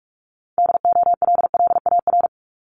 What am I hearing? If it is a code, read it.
Morse code, heard as a keyed tone, about 46 wpm, 700 Hz. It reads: DOLLAR
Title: Multi-letter Phonemes Repeated 3x 45wpm